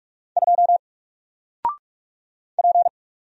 Morse code, heard as a keyed tone, about 44 wpm, 700 Hz. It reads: 2 P
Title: Single Character (Letter:Number:Punctuation:Prosign) Alt 45wpm